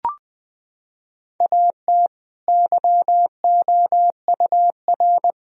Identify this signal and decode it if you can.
Morse code, heard as a keyed tone, about 21 wpm, 700 Hz. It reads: AT YOUR
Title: Sentences from Top 1000 Words - Review Part 4 - 20wpm